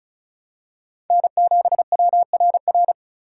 Morse code, heard as a keyed tone, about 36 wpm, 700 Hz. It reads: N7WRR
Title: USA Call Signs - 5 Characters - 3x Character Spacing - 35wpm